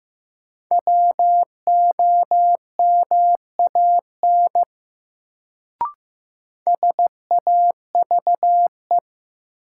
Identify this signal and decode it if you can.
Morse code, heard as a keyed tone, about 15 wpm, 700 Hz. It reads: WOMAN SAVE
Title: Sentences from Top 1100 Words - New Word Drill Alt - 15wpm